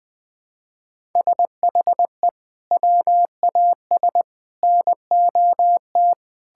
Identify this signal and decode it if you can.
Morse code, heard as a keyed tone, about 20 wpm, 700 Hz. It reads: SHE WAS NOT
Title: Sentences with 6-Word Phrases - 20wpm